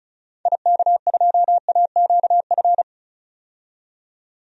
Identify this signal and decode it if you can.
Morse code, heard as a keyed tone, about 36 wpm, 700 Hz. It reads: IK2AQF
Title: DXCC-100 Call Signs - 6 Characters - 35wpm